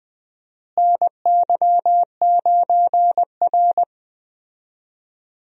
Morse code, heard as a keyed tone, about 20 wpm, 700 Hz. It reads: NY9R
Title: USA Call Signs - 4 Characters - Repeated 3x - 20wpm